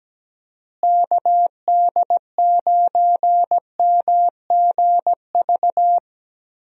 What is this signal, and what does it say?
Morse code, heard as a keyed tone, about 17 wpm, 700 Hz. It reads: KD9MGV
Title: USA Call Signs - 6 Characters - Repeated 2x - 17wpm